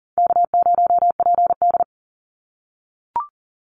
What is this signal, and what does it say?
Morse code, heard as a keyed tone, about 38 wpm, 700 Hz. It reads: K0PD
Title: Easy Call Signs - 40wpm